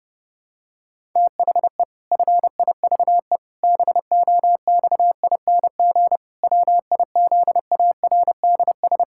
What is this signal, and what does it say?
Morse code, heard as a keyed tone, about 30 wpm, 700 Hz. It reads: THE FIVE BOXING WIZARDS
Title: Pangram Challenge v2 - 30wpm